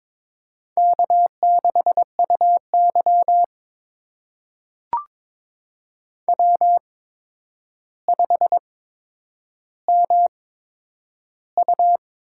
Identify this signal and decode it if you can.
Morse code, heard as a keyed tone, about 22 wpm, spaced as for 9 wpm, 700 Hz. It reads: K6UY W5MU
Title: USA Call Signs - 4 Characters - 8x Character Spacing - 22wpm